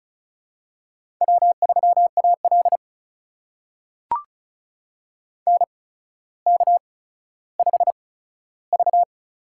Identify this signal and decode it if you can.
Morse code, heard as a keyed tone, about 36 wpm, spaced as for 15 wpm, 700 Hz. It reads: W3AL NK5V
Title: USA Call Signs - 4 Characters - 8x Character Spacing - 35wpm